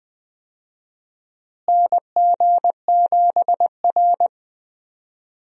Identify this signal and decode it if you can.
Morse code, heard as a keyed tone, about 20 wpm, 700 Hz. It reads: NG7R
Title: USA Call Signs - 4 Characters - Repeated 2x - 20wpm